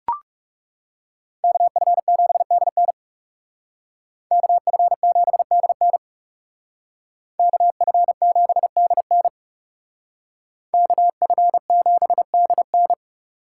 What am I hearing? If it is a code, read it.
Morse code, heard as a keyed tone, about 40 wpm, 700 Hz. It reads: KF7DN KF7DN KF7DN KF7DN